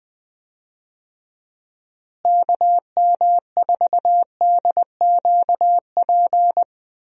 Morse code, heard as a keyed tone, about 19 wpm, 700 Hz. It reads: KM4DQP
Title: USA Call Signs - 6 Characters - Repeated 3x - 20wpm